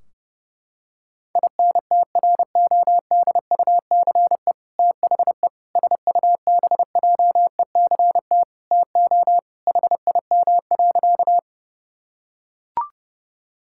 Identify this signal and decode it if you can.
Morse code, heard as a keyed tone, about 30 wpm, 700 Hz. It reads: INTRODUCE THE SUBJECT TO HIM.
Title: Sentences from Top 1600 Words 30wpm